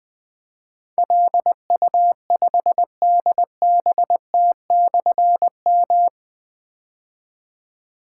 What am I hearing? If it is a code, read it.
Morse code, heard as a keyed tone, about 20 wpm, 700 Hz. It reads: LU5DBT/M
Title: Difficult Call Signs 20wpm